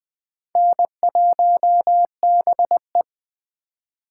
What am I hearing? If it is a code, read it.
Morse code, heard as a keyed tone, about 21 wpm, 700 Hz. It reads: N1BE
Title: USA Call Signs - 3x Character Spacing - 20wpm